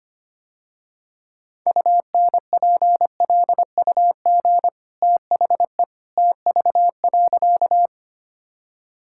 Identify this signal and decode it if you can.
Morse code, heard as a keyed tone, about 25 wpm, 700 Hz. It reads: UNPLUG THE TV.
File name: Easy Sentences with Hints - 25wpm